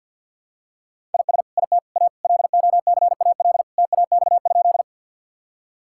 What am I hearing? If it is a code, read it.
Morse code, heard as a keyed tone, about 46 wpm, 690 Hz. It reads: IS IT A LOCAL TAX?
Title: Sentences from Top 1000 Words - Review Part 5 - 50wpm